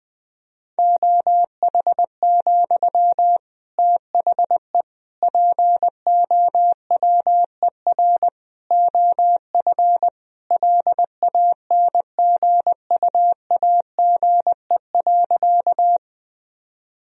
Morse code, heard as a keyed tone, about 20 wpm, 700 Hz. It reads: OH, THE POWER OF LANGUAGE.
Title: Sentences from Top 1000 Words - Review Part 3 - 20wpm